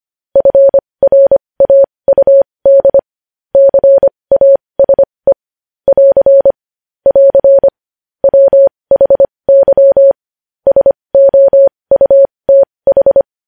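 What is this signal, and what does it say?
Morse code, heard as a keyed tone, about 25 wpm, spaced as for 20 wpm, 550 Hz. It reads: FRAUD CASE <AR> <AR> WHY SOUTH
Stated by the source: Morse code news headlines practice at 20 wpm.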